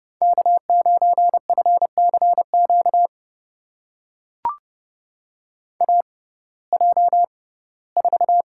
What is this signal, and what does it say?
Morse code, heard as a keyed tone, about 32 wpm, spaced as for 19 wpm, 700 Hz. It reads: K9FCQ AJ4
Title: USA Call Signs - 5 Characters - 6x Character Spacing - 30wpm